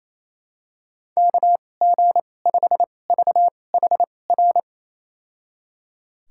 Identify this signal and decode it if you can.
Morse code, heard as a keyed tone, about 28 wpm, spaced as for 20 wpm, 700 Hz. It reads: KG5VHR
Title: USA Call Signs - 6 Characters - 2x Character Spacing - 28wpm